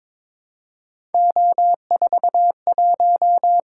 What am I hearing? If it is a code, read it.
Morse code, heard as a keyed tone, about 22 wpm, 700 Hz. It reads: O41
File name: practice / 22wpm / Three Letter-Number Combinations - 22wpm